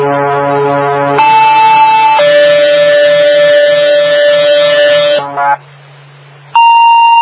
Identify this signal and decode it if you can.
Two-tone page, heard as a one-second tone, then a three-second tone, about 880 then 580 Hz, with noise. It is heard from 1.2 to 5.2 s.